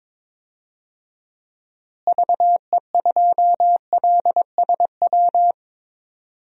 Morse code, heard as a keyed tone, about 22 wpm, 700 Hz. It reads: VE2LSW
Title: DXCC-100 Call Signs - 6 Characters - Repeated 2x - 22wpm